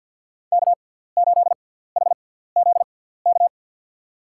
Morse code, heard as a keyed tone, about 53 wpm, spaced as for 17 wpm, 700 Hz. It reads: K8HZK
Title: USA Call Signs - 5 Characters - 6x Character Spacing - 50wpm